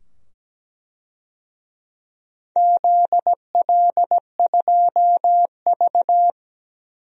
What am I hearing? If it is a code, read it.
Morse code, heard as a keyed tone, about 17 wpm, 700 Hz. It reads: ZL2V
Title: DXCC-100 Call Signs - 4 Characters - 17wpm